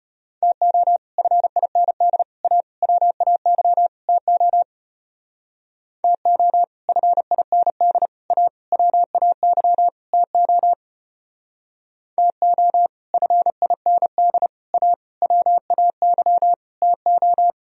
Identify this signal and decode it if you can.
Morse code, heard as a keyed tone, about 32 wpm, 700 Hz. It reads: TO FIND A WAY TO TO FIND A WAY TO TO FIND A WAY TO